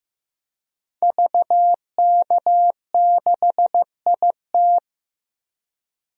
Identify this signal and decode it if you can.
Morse code, heard as a keyed tone, about 15 wpm, 700 Hz. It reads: VK6IT
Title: DXCC-100 Call Signs - 5 Characters - Repeated 3x - 15wpm